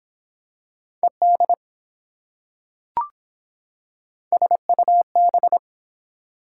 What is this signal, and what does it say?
Morse code, heard as a keyed tone, about 24 wpm, 700 Hz. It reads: ED SUB